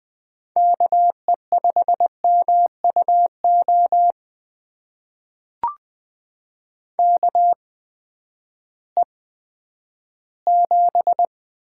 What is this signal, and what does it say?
Morse code, heard as a keyed tone, about 20 wpm, spaced as for 11 wpm, 700 Hz. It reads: KE5MUO KE7
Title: USA Call Signs - 6 Characters - 8x Character Spacing - 20wpm